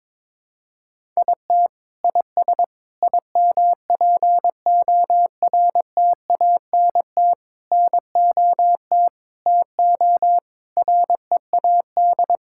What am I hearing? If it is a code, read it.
Morse code, heard as a keyed tone, about 22 wpm, 700 Hz. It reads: IT IS IMPORTANT NOT TO READ
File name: Sets of 9 Words - 22wpm